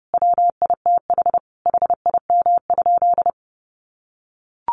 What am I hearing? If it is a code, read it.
Morse code, heard as a keyed tone, about 30 wpm, 700 Hz. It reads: WITH HIM?
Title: Sentences from Top 600 Words 30wpm